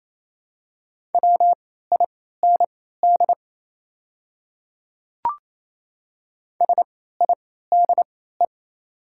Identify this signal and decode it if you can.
Morse code, heard as a keyed tone, about 27 wpm, spaced as for 16 wpm, 700 Hz. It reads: WIND SIDE
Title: 3x Character Spacing - Top 500 Words - 28wpm